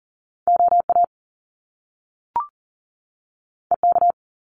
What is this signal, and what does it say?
Morse code, heard as a keyed tone, about 38 wpm, 700 Hz. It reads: OA EK
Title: Top Two Letter Combinations in Words 40wpm